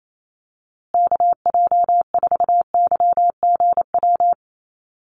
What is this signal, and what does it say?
Morse code, heard as a keyed tone, about 28 wpm, 700 Hz. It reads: KJ4YGW
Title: USA Call Signs Repeated 3x 28wpm